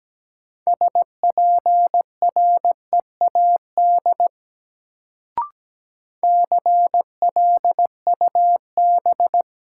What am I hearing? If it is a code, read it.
Morse code, heard as a keyed tone, about 17 wpm, 700 Hz. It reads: SPREAD CLUB
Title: Sentences from Top 1200 Words - New Word Drill - 17wpm